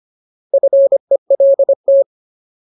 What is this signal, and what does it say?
Morse code, heard as a keyed tone, about 25 wpm, 550 Hz. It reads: FELT
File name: Ultimate Word List / 25WPM Ultimate